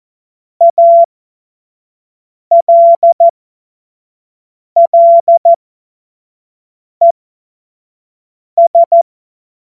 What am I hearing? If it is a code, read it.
Morse code, heard as a keyed tone, about 14 wpm, spaced as for 5 wpm, 680 Hz. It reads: ALLES
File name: Morsekurs / Witze / Witz 27. Juni 2023 25 F70